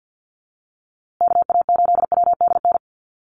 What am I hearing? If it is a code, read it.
Morse code, heard as a keyed tone, about 50 wpm, 700 Hz. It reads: KA8WDN
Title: USA Call Signs Repeated 3x 50wpm